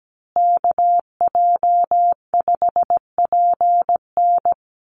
Morse code, heard as a keyed tone, about 17 wpm, 700 Hz. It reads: KJ5PN
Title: USA Call Signs Repeated 2x 17wpm